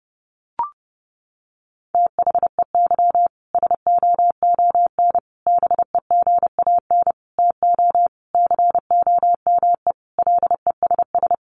Morse code, heard as a keyed tone, about 28 wpm, 700 Hz. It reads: THEY SOON BEGAN TO COME LESS
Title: Sentences from Top 1300 Words 30wpm